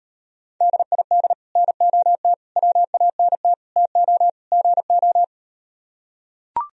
Morse code, heard as a keyed tone, about 38 wpm, 700 Hz. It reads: DID NOT WANT TO GO